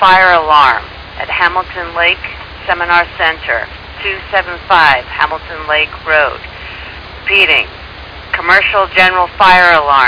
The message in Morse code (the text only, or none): none